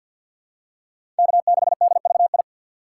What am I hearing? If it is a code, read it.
Morse code, heard as a keyed tone, about 48 wpm, 700 Hz. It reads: K6DVI
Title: USA Call Signs - 3x Character Spacing - 50wpm